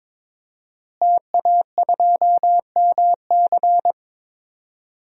Morse code, heard as a keyed tone, about 22 wpm, 700 Hz. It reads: TA2MC
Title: DXCC-100 Call Signs - 5 Characters - Repeated 3x - 22wpm